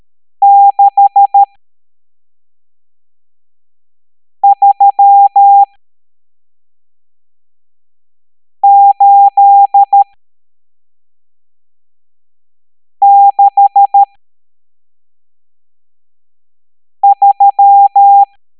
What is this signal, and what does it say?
Morse code, heard as a keyed tone, about 13 wpm, spaced as for 3 wpm, 800 Hz. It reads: 63863
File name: hamdata / L12-EX02-CHIFFRES